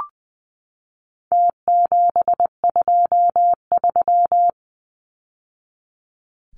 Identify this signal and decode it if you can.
Morse code, heard as a keyed tone, about 20 wpm, 700 Hz. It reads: T723
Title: Numbers - Contest Style - 4-Digit 20wpm